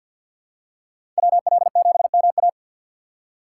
Morse code, heard as a keyed tone, about 53 wpm, 700 Hz. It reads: WL7MA
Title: USA Call Signs - 5 Characters - 2x Character Spacing - 50wpm